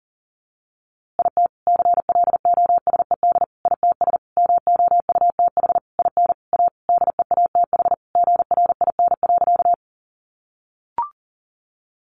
Morse code, heard as a keyed tone, about 40 wpm, 700 Hz. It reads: IT CLOSED ITS MOUTH IN A DEATH GRIN.
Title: A Sound Of Thunder 40wpm